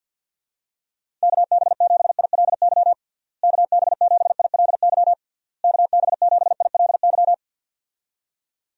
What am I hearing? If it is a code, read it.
Morse code, heard as a keyed tone, about 47 wpm, 700 Hz. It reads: KB7ILY KB7ILY KB7ILY
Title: USA Call Signs - 6 Characters - Repeated 3x - 50wpm